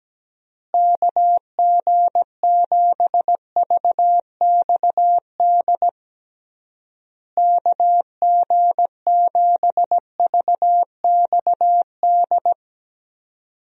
Morse code, heard as a keyed tone, about 17 wpm, 700 Hz. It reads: KG7VXD KG7VXD
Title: USA Call Signs - 6 Characters - Repeated 3x - 17wpm